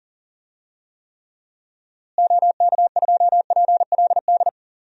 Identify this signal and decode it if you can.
Morse code, heard as a keyed tone, about 40 wpm, 700 Hz. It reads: OK2PLD
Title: DXCC-100 Call Signs - 6 Characters - Repeated 3x - 40wpm